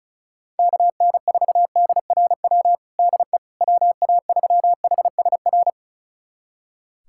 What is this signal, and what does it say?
Morse code, heard as a keyed tone, about 34 wpm, 700 Hz. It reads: KN4DRW DE WA3HSR
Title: USA Callsign DE Callsign 35wpm